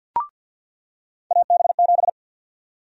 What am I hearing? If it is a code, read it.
Morse code, heard as a keyed tone, about 53 wpm, 700 Hz. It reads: AB7
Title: USA Call Sign Prefixes - 50wpm